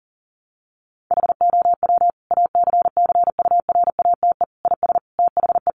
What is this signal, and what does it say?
Morse code, heard as a keyed tone, about 40 wpm, 700 Hz. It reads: HOW ACCURATE IS THE
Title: Sentences from Top 1200 Words 40wpm